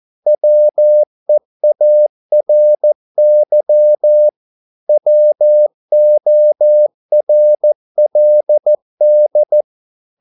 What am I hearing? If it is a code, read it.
Morse code, heard as a keyed tone, about 14 wpm, 600 Hz. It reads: WEARY WORLD
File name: mp3 files / a dogs mistake - 14wpm